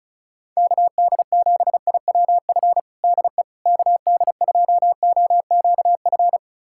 Morse code, heard as a keyed tone, about 36 wpm, 690 Hz. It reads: KD7IWF DE KD2OQF
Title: USA Callsign DE Callsign - 6 Characters - 35wpm